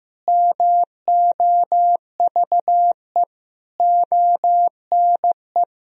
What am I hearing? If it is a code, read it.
Morse code, heard as a keyed tone, about 15 wpm, 700 Hz. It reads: MOVE ONE
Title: Sentences from Top 1000 Words - Review Part 3 - 15wpm